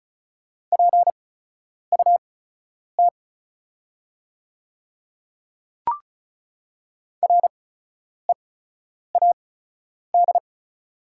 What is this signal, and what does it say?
Morse code, heard as a keyed tone, about 34 wpm, spaced as for 10 wpm, 690 Hz. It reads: PUT READ
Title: 8x Character Spacing - Top 500 Words - 35wpm